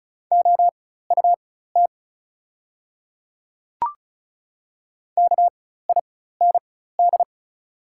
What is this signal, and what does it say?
Morse code, heard as a keyed tone, about 34 wpm, spaced as for 16 wpm, 690 Hz. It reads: OUT KIND
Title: 4x Character Spacing - Top 300 Words - 35wpm